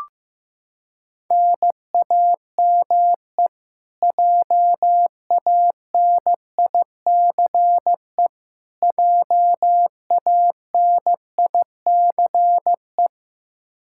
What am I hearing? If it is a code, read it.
Morse code, heard as a keyed tone, about 16 wpm, 700 Hz. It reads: NAME JANICE JANICE
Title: Practice QSO - Exchange 1 - Beginning - 15wpm